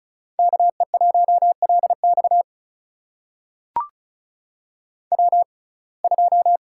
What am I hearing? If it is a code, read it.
Morse code, heard as a keyed tone, about 36 wpm, spaced as for 25 wpm, 700 Hz. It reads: KE1LX W2
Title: USA Call Signs - 5 Characters - 6x Character Spacing - 35wpm